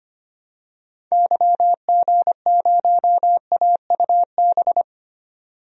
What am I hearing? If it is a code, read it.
Morse code, heard as a keyed tone, about 25 wpm, 700 Hz. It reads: YG0AUB
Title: DXCC-100 Call Signs - 6 Characters - 25wpm